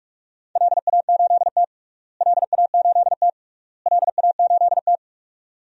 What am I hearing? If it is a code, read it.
Morse code, heard as a keyed tone, about 42 wpm, 700 Hz. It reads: LA8T LA8T LA8T
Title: DXCC-100 Call Signs - 4 Characters - Repeated 3x - 45wpm